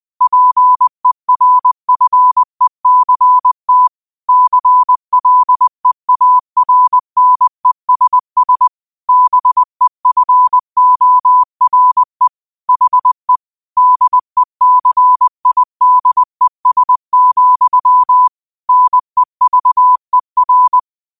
Morse code, heard as a keyed tone, about 20 wpm, 1000 Hz. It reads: PERFECT CLEARNESS BEFORE HE DECIDES, NEVER
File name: qotdfiles / 20wpm